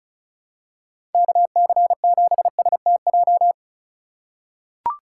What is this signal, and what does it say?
Morse code, heard as a keyed tone, about 34 wpm, 700 Hz. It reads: KC7STJ
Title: USA Call Signs - 6 Characters - 4x Character Spacing - 35wpm